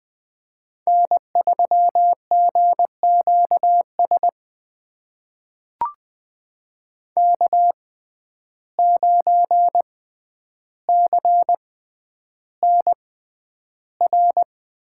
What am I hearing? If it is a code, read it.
Morse code, heard as a keyed tone, about 20 wpm, spaced as for 10 wpm, 700 Hz. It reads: N3GQS K9CNR
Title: USA Call Signs - 5 Characters - 6x Character Spacing - 20wpm